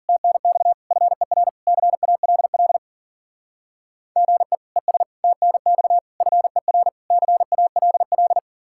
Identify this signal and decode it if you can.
Morse code, heard as a keyed tone, about 49 wpm, 700 Hz. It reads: TNX FER CALL GE ES TNX FER CALL
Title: Speed-Racing Common QSO Phrases - 40wpm